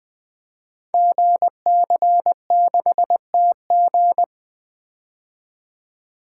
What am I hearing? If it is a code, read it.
Morse code, heard as a keyed tone, about 20 wpm, 700 Hz. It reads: GC6TG
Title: Code Groups - 5 Letter:Numbers - 20wpm